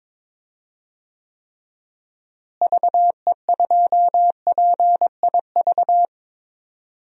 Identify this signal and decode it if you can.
Morse code, heard as a keyed tone, about 22 wpm, 700 Hz. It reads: VE2PIV
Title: DXCC-100 Call Signs - 6 Characters - Repeated 3x - 22wpm